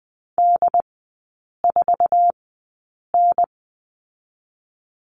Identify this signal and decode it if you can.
Morse code, heard as a keyed tone, about 20 wpm, spaced as for 8 wpm, 700 Hz. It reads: D4N
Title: 074 Lesson 13e - Letter L - Three Characters 20wpm